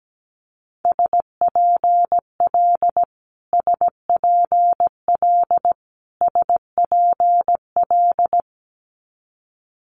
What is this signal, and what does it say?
Morse code, heard as a keyed tone, about 17 wpm, 700 Hz. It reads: SPL SPL SPL
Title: Multi-letter Phonemes Repeated 3x 17wpm